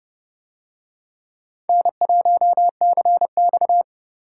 Morse code, heard as a keyed tone, about 28 wpm, 700 Hz. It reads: N1CX